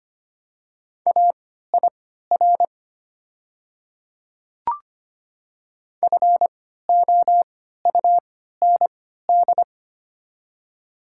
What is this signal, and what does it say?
Morse code, heard as a keyed tone, about 25 wpm, spaced as for 14 wpm, 700 Hz. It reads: AIR FOUND
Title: 3x Character Spacing - Top 300 Words - 25wpm